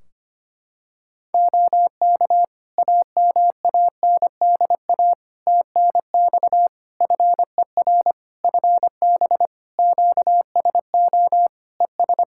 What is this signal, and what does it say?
Morse code, heard as a keyed tone, about 25 wpm, 700 Hz. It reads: OK AMANDA TNX FER FB QSO ES
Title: K1USN SST QSO Practice 25wpm